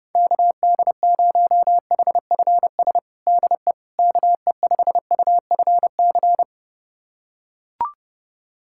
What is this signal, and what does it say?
Morse code, heard as a keyed tone, about 30 wpm, 700 Hz. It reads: KD0HFS DE KE5UFC
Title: USA Callsign DE Callsign - 6 Characters - 30wpm